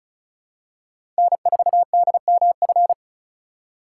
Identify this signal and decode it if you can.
Morse code, heard as a keyed tone, about 36 wpm, 700 Hz. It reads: N4DMF
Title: USA Call Signs - 5 Characters - 4x Character Spacing - 35wpm